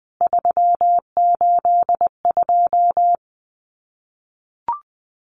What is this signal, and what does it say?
Morse code, heard as a keyed tone, about 20 wpm, 700 Hz. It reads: 382
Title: CWT Exchange 20wpm